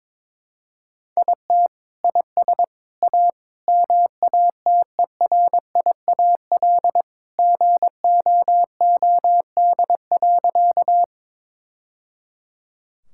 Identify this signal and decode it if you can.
Morse code, heard as a keyed tone, about 22 wpm, 700 Hz. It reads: IT IS A MATERIAL GOOD.
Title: Sentences from Top 1000 Words - Review Part 5 - 22wpm